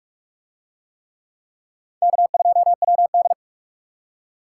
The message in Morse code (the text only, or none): K2WD